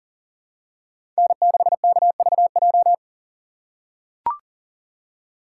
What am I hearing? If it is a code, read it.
Morse code, heard as a keyed tone, about 40 wpm, 700 Hz. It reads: N6KVJ
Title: USA Call Signs - 5 Characters - 6x Character Spacing - 40wpm